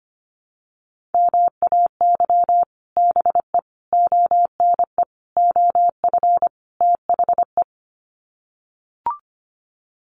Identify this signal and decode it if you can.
Morse code, heard as a keyed tone, about 25 wpm, 700 Hz. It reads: MAY BE ONE OF THE
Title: Sets of 5 Words Encore2 25wpm